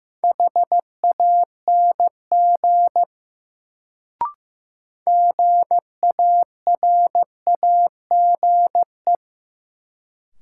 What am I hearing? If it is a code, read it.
Morse code, heard as a keyed tone, about 15 wpm, 700 Hz. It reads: HANG GARAGE
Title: Sentences from Top 1600 Words - New Word Drill Alt - 15wpm